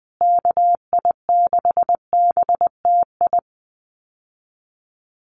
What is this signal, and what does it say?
Morse code, heard as a keyed tone, about 20 wpm, 700 Hz. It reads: KI6BTI
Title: USA Call Signs Repeated 2x 20wpm